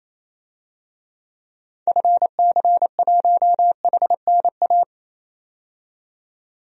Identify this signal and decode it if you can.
Morse code, heard as a keyed tone, about 28 wpm, 700 Hz. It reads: FC1HNA
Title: DXCC-100 Call Signs - 6 Characters - 28wpm